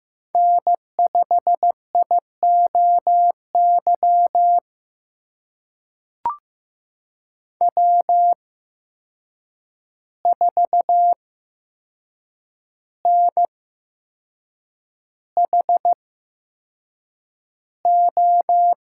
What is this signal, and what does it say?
Morse code, heard as a keyed tone, about 15 wpm, spaced as for 6 wpm, 700 Hz. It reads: N5IOY W4NHO
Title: USA Call Signs - 5 Characters - 8x Character Spacing - 15wpm